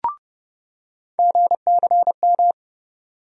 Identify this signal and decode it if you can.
Morse code, heard as a keyed tone, about 31 wpm, 700 Hz. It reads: GCM
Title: USA Call Sign Postfixes - 3 Letters 30wpm